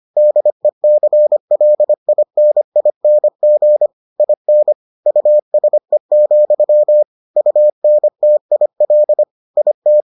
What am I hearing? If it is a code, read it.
Morse code, heard as a keyed tone, about 25 wpm, 600 Hz. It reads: DECLINING IN USE, UNTIL IT